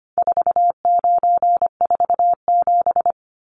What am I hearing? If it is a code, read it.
Morse code, heard as a keyed tone, about 25 wpm, 700 Hz. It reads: 4947
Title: Numbers - 4-Digit 25wpm